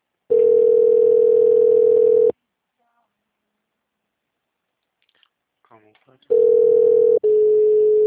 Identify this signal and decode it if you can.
Ringback tone (North America), 440 + 480 Hz, with speech.
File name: out- -2038- - - .33640